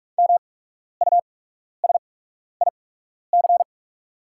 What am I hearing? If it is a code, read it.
Morse code, heard as a keyed tone, about 44 wpm, spaced as for 12 wpm, 700 Hz. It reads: MUSIC
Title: 8x Character Spacing - Top 300 Words - 45wpm